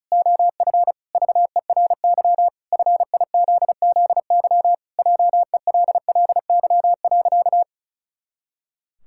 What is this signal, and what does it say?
Morse code, heard as a keyed tone, about 36 wpm, 700 Hz. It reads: OF VERY FIZZY JELLY.
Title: Pangram Challenge v2 - 35wpm